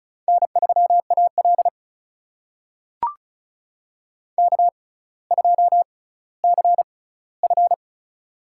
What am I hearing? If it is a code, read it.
Morse code, heard as a keyed tone, about 36 wpm, spaced as for 18 wpm, 700 Hz. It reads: N3AL K2CF
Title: USA Call Signs - 4 Characters - 6x Character Spacing - 35wpm